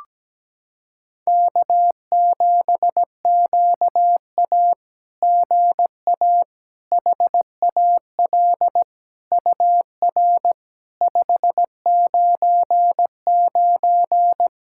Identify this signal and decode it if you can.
Morse code, heard as a keyed tone, about 17 wpm, 700 Hz. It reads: K7QA GA HAL UR 599
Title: SOTA QSO Practice 17wpm